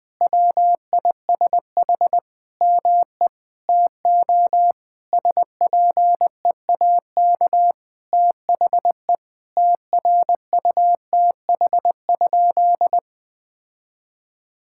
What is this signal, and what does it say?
Morse code, heard as a keyed tone, about 20 wpm, 700 Hz. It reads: WISH ME TO SPEAK THE TRUTH?
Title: Sentences from Top 1600 Words 20wpm